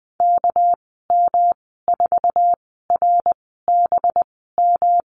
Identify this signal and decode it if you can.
Morse code, heard as a keyed tone, about 20 wpm, spaced as for 14 wpm, 700 Hz. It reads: KM4RBM
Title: 245 Lesson 39g - Letter z - Call Signs 20wpm